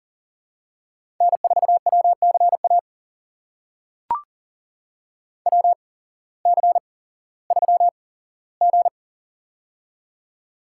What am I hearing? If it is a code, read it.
Morse code, heard as a keyed tone, about 40 wpm, spaced as for 19 wpm, 700 Hz. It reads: N4WCA WC3G
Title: USA Call Signs - 8x Character Spacing - 40wpm